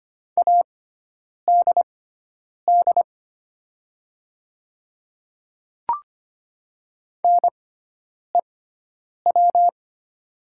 Morse code, heard as a keyed tone, about 25 wpm, spaced as for 9 wpm, 700 Hz. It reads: ADD NEW
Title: 6x Character Spacing - Top 500 Words - 25wpm